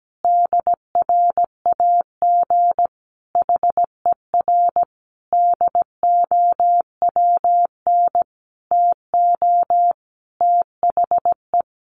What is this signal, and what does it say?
Morse code, heard as a keyed tone, about 17 wpm, 700 Hz. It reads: DRAG HER DOWN TO THE
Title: Sentences from Top 1500 Words 17wpm